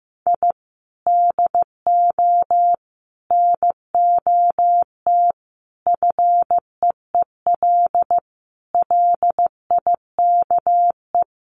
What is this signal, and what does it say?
Morse code, heard as a keyed tone, about 14 wpm, 700 Hz. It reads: I DO NOT FEEL LIKE
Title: Sets of 5 Words Encore 15wpm